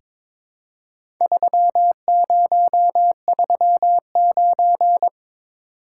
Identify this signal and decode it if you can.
Morse code, heard as a keyed tone, about 22 wpm, 700 Hz. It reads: 3039
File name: practice / 22wpm / Four Letter-Number Combinations - 22wpm